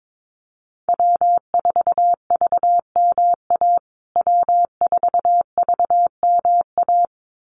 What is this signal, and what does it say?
Morse code, heard as a keyed tone, about 22 wpm, 700 Hz. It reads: W4VMA W4VMA
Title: USA Call Signs Repeated 2x 22wpm